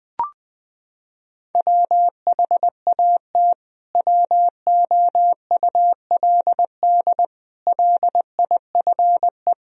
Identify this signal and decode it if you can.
Morse code, heard as a keyed tone, about 20 wpm, 700 Hz. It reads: WHAT WOULD LIFE
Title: Sentences from Top 1000 Words - Review Part 3 - 20wpm